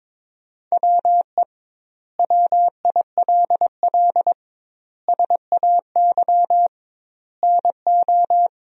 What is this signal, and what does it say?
Morse code, heard as a keyed tone, about 22 wpm, 700 Hz. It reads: WE WILL SAY NO
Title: 2x Word-Spacing - Sentences from Top 200 Words - 22wpm